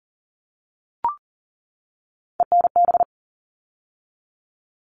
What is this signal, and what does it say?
Morse code, heard as a keyed tone, about 40 wpm, 700 Hz. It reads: ENB
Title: All Cut Numbers - 3-Digit 40wpm